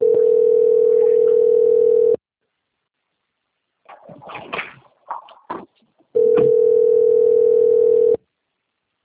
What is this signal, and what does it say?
Ringback tone (North America), 440 + 480 Hz.